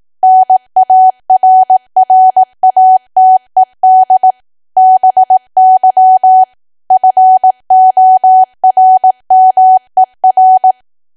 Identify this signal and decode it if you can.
Morse code, heard as a keyed tone, about 18 wpm, 750 Hz. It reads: NARRATED BY FORMER
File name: Archive / 18 WPM / 18WPM